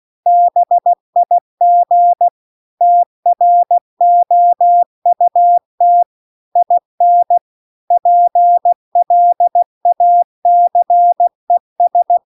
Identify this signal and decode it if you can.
Morse code, heard as a keyed tone, about 16 wpm, 700 Hz. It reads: BIG TROUT IN PLACES
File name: CW-The-Big-Twoheaded-River-2-16WPM